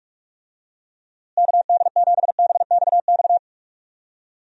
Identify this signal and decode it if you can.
Morse code, heard as a keyed tone, about 46 wpm, 700 Hz. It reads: KD7BXX
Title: USA Call Signs - 6 Characters - Repeated 3x - 45wpm